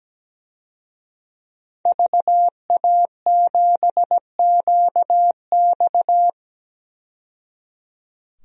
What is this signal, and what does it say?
Morse code, heard as a keyed tone, about 17 wpm, 700 Hz. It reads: VA7QX
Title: DXCC-100 Call Signs - 5 Characters - 17wpm